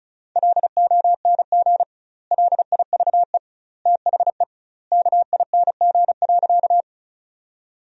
Morse code, heard as a keyed tone, about 36 wpm, 700 Hz. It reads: LONG LIVE THE KING.
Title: 2x Word-Spacing - Sentences from Top 400 Words - 35wpm